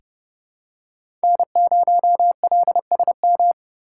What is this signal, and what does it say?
Morse code, heard as a keyed tone, about 32 wpm, 700 Hz. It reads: N0LSM
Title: USA Call Signs - 5 Characters - 2x Character Spacing - 30wpm